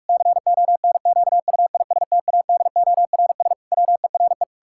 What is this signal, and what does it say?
Morse code, heard as a keyed tone, about 46 wpm, 700 Hz. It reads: CONQUISTADORS WERE
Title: Pangram Challenge v2 - 45wpm